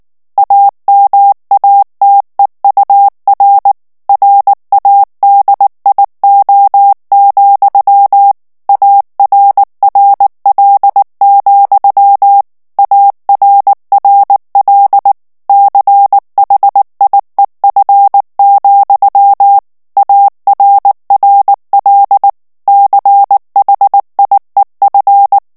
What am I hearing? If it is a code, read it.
Morse code, heard as a keyed tone, about 19 wpm, 800 Hz. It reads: AMATEUR RADIO, ARRL, ARRL CHIEF, ARRL CHIEF
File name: morse mp3 / ANGLAIS texte clair 3 vitesse 18 mots minute